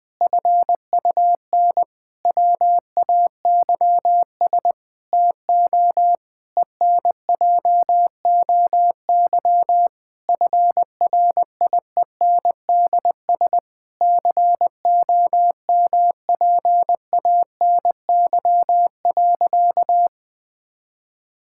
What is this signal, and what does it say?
Morse code, heard as a keyed tone, about 20 wpm, 700 Hz. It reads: FUN WAYS TO ENJOY FRIENDS COMPANY.
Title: Difficult Sentences with Hints - 20wpm